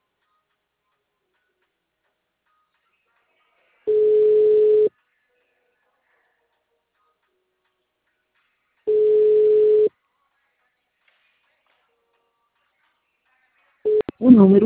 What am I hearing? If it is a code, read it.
Ringback tone (Europe, 425 Hz), with speech.